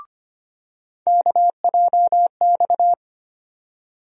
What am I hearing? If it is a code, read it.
Morse code, heard as a keyed tone, about 25 wpm, 700 Hz. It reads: KJX
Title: USA Call Sign Postfixes - 3 Letters 25wpm